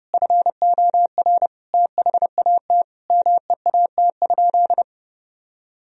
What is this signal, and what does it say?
Morse code, heard as a keyed tone, about 32 wpm, 700 Hz. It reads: FOR THAT MEAT?
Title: Sentences from Top 1000 Words - Review Part 3 - 30wpm